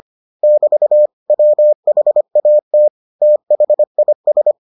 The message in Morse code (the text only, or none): <BT> WHAT THIS